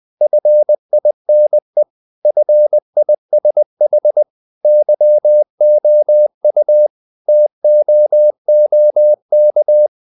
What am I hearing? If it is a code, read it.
Morse code, heard as a keyed tone, about 20 wpm, 600 Hz. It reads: FINE FISH YOU TOOK
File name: the old man and the sea part 2 20wpm